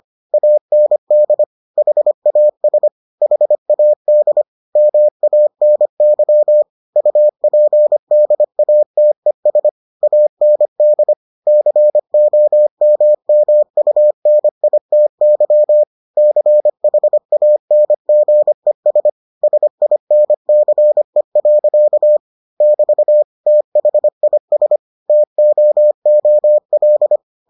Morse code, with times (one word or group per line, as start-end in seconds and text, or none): 0.3-1.4 AND
1.8-2.9 HAS
3.2-4.4 HAD
4.8-6.6 MANY
7.0-9.7 UPDATES
10.0-11.1 AND
11.5-15.8 COMMUNITY
16.2-19.1 CHANGES
19.4-22.2 SINCE.
22.6-23.2 <BT>
23.5-24.8 THIS
25.1-27.2 TOOL